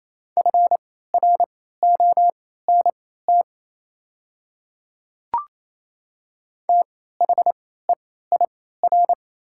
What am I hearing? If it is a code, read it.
Morse code, heard as a keyed tone, about 28 wpm, spaced as for 16 wpm, 700 Hz. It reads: FRONT THEIR
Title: 3x Character Spacing - Top 500 Words - 28wpm